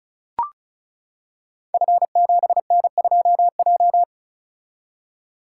Morse code, heard as a keyed tone, about 36 wpm, 700 Hz. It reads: F7N2J
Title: Code Groups - 5 Letter:Numbers - 35wpm